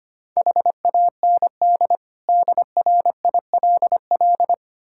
Morse code, heard as a keyed tone, about 25 wpm, 700 Hz. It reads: HAND DRILL
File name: Easy Words - Speed-Racing with Hints - 25wpm